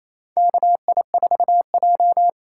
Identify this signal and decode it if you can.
Morse code, heard as a keyed tone, about 28 wpm, 700 Hz. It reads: KI4J
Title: USA Call Signs - 4 Characters - 2x Character Spacing - 28wpm